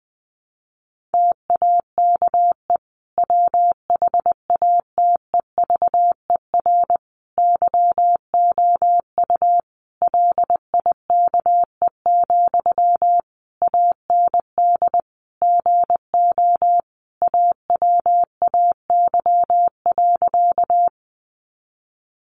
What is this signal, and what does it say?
Morse code, heard as a keyed tone, about 20 wpm, 700 Hz. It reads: TAKE WHATEVER YOU LIKE, AND GO AWAY.
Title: Sentences from Top 1100 Words 20wpm